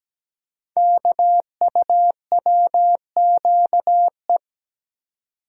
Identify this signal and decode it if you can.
Morse code, heard as a keyed tone, about 17 wpm, 700 Hz. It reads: KUWQE
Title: Code Groups - 5 Letters - 17wpm